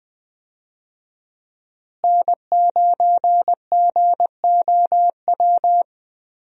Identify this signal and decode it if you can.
Morse code, heard as a keyed tone, about 20 wpm, 700 Hz. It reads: N9GOW
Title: USA Call Signs - 5 Characters - Repeated 2x - 20wpm